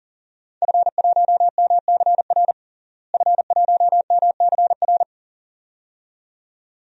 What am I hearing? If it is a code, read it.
Morse code, heard as a keyed tone, about 40 wpm, 700 Hz. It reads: F1MCR F1MCR
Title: DXCC-100 Call Signs - 5 Characters - Repeated 2x - 40wpm